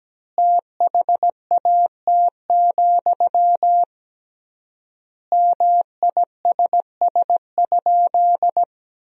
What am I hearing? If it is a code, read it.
Morse code, heard as a keyed tone, about 17 wpm, 700 Hz. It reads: THAT, MISS?
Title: 3x Word-Spacing - Sentences from Top 500 Words - 17wpm